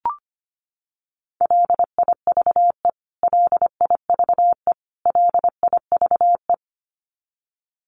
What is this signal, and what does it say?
Morse code, heard as a keyed tone, about 25 wpm, 700 Hz. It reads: LIVE LIVE LIVE
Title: Top 500 Words Repeated 25wpm